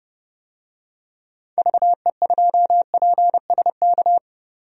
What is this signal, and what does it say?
Morse code, heard as a keyed tone, about 30 wpm, 700 Hz. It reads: VE2PSK
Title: DXCC-100 Call Signs - 6 Characters - 30wpm